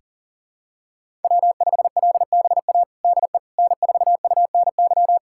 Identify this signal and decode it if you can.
Morse code, heard as a keyed tone, about 40 wpm, 700 Hz. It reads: W5LBA DE N4UNY
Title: USA Callsign DE Callsign - 5 Characters - 40wpm